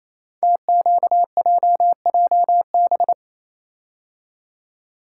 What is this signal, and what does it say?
Morse code, heard as a keyed tone, about 29 wpm, 700 Hz. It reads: TQJJB
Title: Code Groups - 5 Letter:Numbers - 28wpm